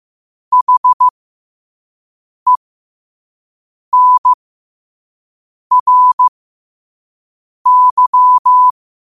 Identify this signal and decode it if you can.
Morse code, heard as a keyed tone, about 15 wpm, spaced as for 5 wpm, 1000 Hz. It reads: HENRY
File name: qotdfiles / 5wpm